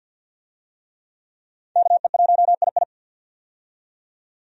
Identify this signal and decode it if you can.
Morse code, heard as a keyed tone, about 50 wpm, 700 Hz. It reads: KE1II
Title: USA Call Signs - 5 Characters - 50wpm